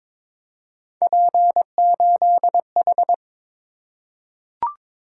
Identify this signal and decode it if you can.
Morse code, heard as a keyed tone, about 22 wpm, 700 Hz. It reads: P8H
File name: practice / 22wpm / Three Letter-Number Combinations - 22wpm